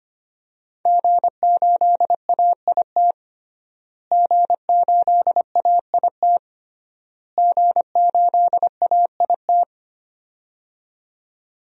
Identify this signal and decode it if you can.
Morse code, heard as a keyed tone, about 25 wpm, 700 Hz. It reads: G8AIT G8AIT G8AIT
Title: DXCC-100 Call Signs - 5 Characters - Repeated 3x - 25wpm